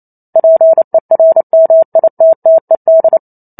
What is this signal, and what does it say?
Morse code, heard as a keyed tone, about 28 wpm, 650 Hz. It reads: PERMITTED